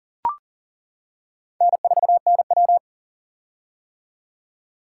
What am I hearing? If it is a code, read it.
Morse code, heard as a keyed tone, about 40 wpm, 700 Hz. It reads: N4NW
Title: K1USN SST QSO Practice - 40wpm